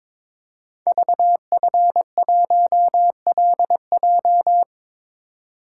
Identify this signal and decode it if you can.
Morse code, heard as a keyed tone, about 22 wpm, 700 Hz. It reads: VF1LJ
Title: Code Groups - 5 Letter:Numbers - 22wpm